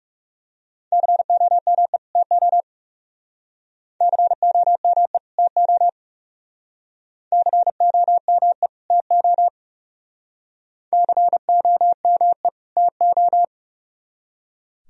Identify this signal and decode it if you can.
Morse code, heard as a keyed tone, about 37 wpm, 700 Hz. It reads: COME TO COME TO COME TO COME TO
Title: Speed-Racing - Sets of 2 Words 30wpm